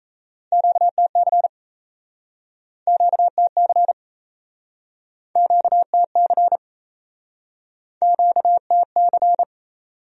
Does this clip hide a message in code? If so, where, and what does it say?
Yes: Morse code from 0.5 to 9.4 s, heard as a keyed tone, about 40 wpm, 700 Hz. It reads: QTC QTC QTC QTC